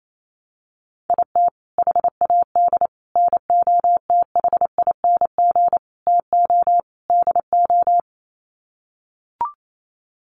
Morse code, heard as a keyed tone, about 28 wpm, 700 Hz. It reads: IT HAD NOTHING TO DO
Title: Sets of 5 Words Encore2 28wpm